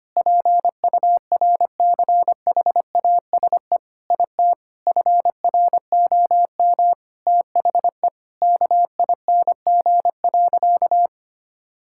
Morse code, heard as a keyed tone, about 25 wpm, 700 Hz. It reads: PURCHASE IT FROM THE KING.
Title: Sentences from Top 1000 Words - Review Part 5 - 25wpm